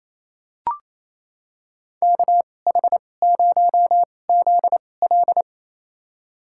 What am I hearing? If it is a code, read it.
Morse code, heard as a keyed tone, about 28 wpm, spaced as for 20 wpm, 700 Hz. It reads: KH0ZL
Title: USA Call Signs - 5 Characters - 2x Character Spacing - 28wpm